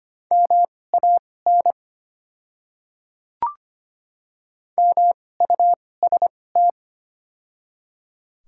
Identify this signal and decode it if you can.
Morse code, heard as a keyed tone, about 25 wpm, spaced as for 18 wpm, 700 Hz. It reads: MAN MUST
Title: 2x Character Spacing - Top 200 Words - 25wpm